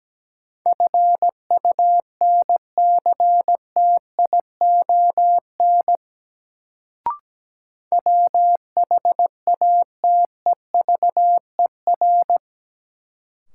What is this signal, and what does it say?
Morse code, heard as a keyed tone, about 17 wpm, 700 Hz. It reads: FUNCTION WHATEVER
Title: Sentences from Top 1100 Words - New Word Drill - 17wpm